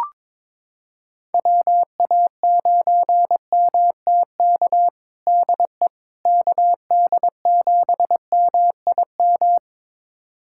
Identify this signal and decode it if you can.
Morse code, heard as a keyed tone, about 22 wpm, 700 Hz. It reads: WA9MTK DE KD7MIM
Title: USA Callsign DE Callsign - 6 Characters - 22wpm